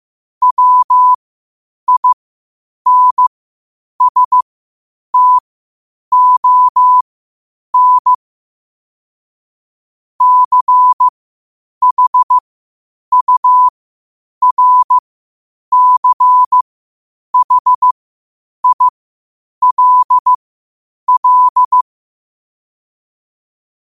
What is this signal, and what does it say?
Morse code, heard as a keyed tone, about 15 wpm, spaced as for 8 wpm, 1000 Hz. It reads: WINSTON CHURCHILL
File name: qotdfiles / 8wpm